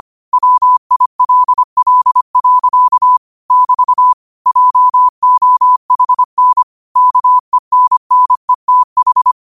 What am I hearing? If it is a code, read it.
Morse code, heard as a keyed tone, about 25 wpm, 1000 Hz. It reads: WILL. <BT> JOHN KENNETH
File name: qotdfiles / 25wpm